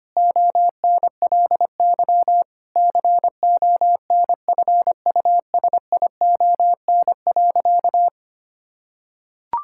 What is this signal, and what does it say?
Morse code, heard as a keyed tone, about 25 wpm, 700 Hz. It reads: ONLY CONFUSION.
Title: Sentences from Top 1600 Words 25wpm